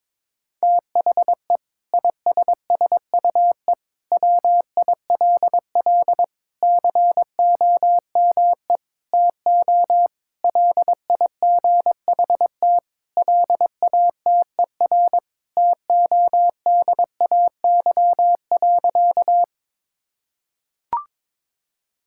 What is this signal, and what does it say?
Morse code, heard as a keyed tone, about 22 wpm, 700 Hz. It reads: THE ISSUE WILL COME TO LIGHT LATER TODAY.
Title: Sentences from Top 1000 Words - Review Part 6 - 22wpm